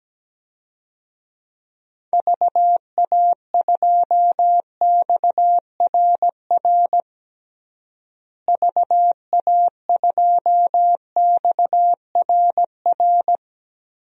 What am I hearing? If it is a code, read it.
Morse code, heard as a keyed tone, about 17 wpm, 700 Hz. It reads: VA2XRR VA2XRR
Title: DXCC-100 Call Signs - 6 Characters - Repeated 3x - 17wpm